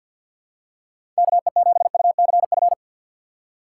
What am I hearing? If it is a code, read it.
Morse code, heard as a keyed tone, about 48 wpm, 700 Hz. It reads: KE7UCF
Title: USA Call Signs - 6 Characters - 6x Character Spacing - 50wpm